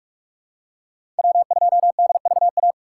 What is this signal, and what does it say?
Morse code, heard as a keyed tone, about 43 wpm, 700 Hz. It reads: W2DVA
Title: USA Call Signs - 5 Characters - 2x Character Spacing - 45wpm